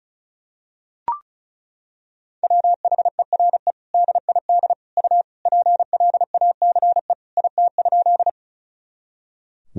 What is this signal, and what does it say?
Morse code, heard as a keyed tone, about 36 wpm, 690 Hz. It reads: WHERE DID U PLACE IT?
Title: Sentences from Top 200 Words and Abbreviations 35wpm